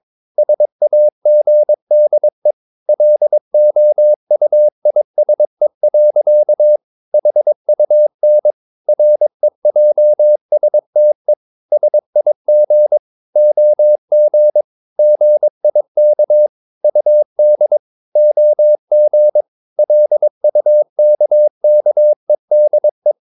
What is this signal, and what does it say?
Morse code, heard as a keyed tone, about 22 wpm, 600 Hz. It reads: SAGDE LOUISE. HUN REJSTE SIG OG GIK UD OG LUKKEDE
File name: Ferie QRS 22wpm